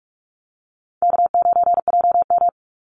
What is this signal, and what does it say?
Morse code, heard as a keyed tone, about 46 wpm, 700 Hz. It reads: K9JM
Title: Easy Call Signs - 45wpm